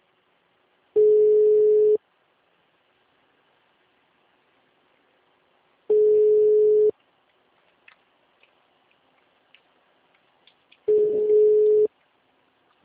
Ringback tone (Europe, 425 Hz).
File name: out- -2029- - - .156